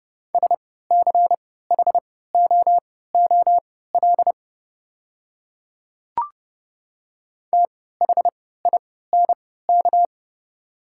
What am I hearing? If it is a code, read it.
Morse code, heard as a keyed tone, about 30 wpm, spaced as for 17 wpm, 690 Hz. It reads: SCHOOL THINK
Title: 3x Character Spacing - Top 500 Words - 30wpm